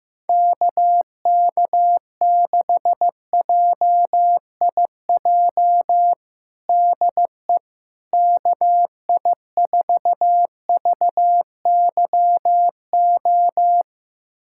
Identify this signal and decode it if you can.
Morse code, heard as a keyed tone, about 15 wpm, 700 Hz. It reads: KK6JIJ DE KI4VYO
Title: USA Callsign DE Callsign - 6 Characters - 15wpm